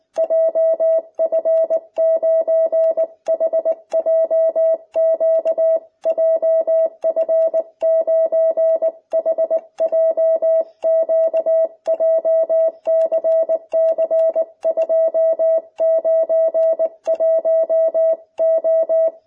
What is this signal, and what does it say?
Morse code, heard as a keyed tone, about 19 wpm, 660 Hz. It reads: JF9HJQ JF9HJQ JCC291O